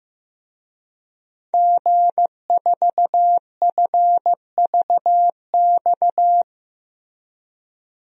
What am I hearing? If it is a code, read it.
Morse code, heard as a keyed tone, about 15 wpm, 700 Hz. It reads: G4FVX
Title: DXCC-100 Call Signs - 5 Characters - Repeated 2x - 15wpm